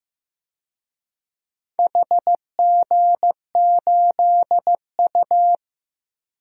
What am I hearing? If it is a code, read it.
Morse code, heard as a keyed tone, about 15 wpm, 700 Hz. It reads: HG8U
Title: DXCC-100 Call Signs - 4 Characters - Repeated 3x - 15wpm